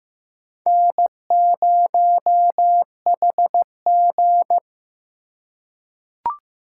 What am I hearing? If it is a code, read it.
Morse code, heard as a keyed tone, about 15 wpm, 700 Hz. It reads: N0HG
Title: USA Call Signs - 4x Character Spacing - 15wpm